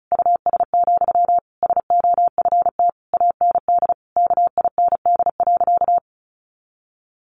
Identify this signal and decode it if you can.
Morse code, heard as a keyed tone, about 34 wpm, 700 Hz. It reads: US, SOFT AND KIND.
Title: Sentences from Top 1500 Words 35wpm